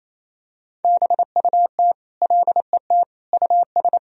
Morse code, heard as a keyed tone, about 27 wpm, 700 Hz. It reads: BUT LET US
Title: Sentences from Top 1000 Words - Review Part 1 - 28wpm